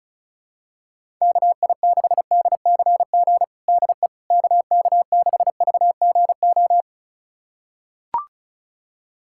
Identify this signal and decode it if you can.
Morse code, heard as a keyed tone, about 34 wpm, 700 Hz. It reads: KI6DCG DE KK6VGO
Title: USA Callsign DE Callsign - 6 Characters - 35wpm